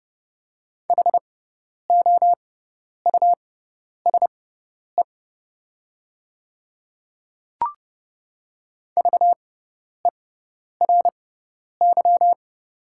Morse code, heard as a keyed tone, about 32 wpm, spaced as for 10 wpm, 700 Hz. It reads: HOUSE VERY
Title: 6x Character Spacing - Top 300 Words - 30wpm